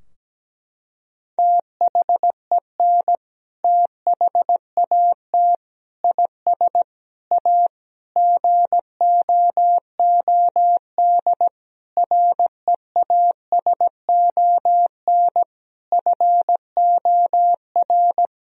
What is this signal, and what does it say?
Morse code, heard as a keyed tone, about 17 wpm, 700 Hz. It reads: THEN THAT IS A GOOD REASON FOR
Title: Sentences from Top 1000 Words - Review Part 4 - 17wpm